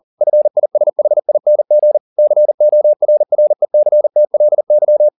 Morse code, heard as a keyed tone, about 40 wpm, 600 Hz. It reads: FISHING CORRECTLY